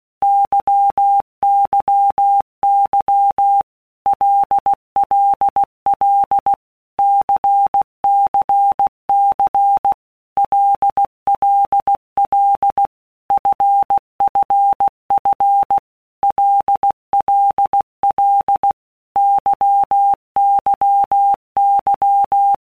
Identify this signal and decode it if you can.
Morse code, heard as a keyed tone, about 16 wpm, 800 Hz. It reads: YYY LLL CCC LLL FFF LLL YYY
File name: Basic Letter Groups By Blocks / Groups@15 / LFCYQ@15